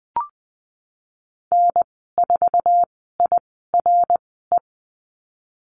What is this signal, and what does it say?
Morse code, heard as a keyed tone, about 20 wpm, spaced as for 14 wpm, 700 Hz. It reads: N4IRE
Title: 055 Lesson 10g - Letter r - Call Signs 20wpm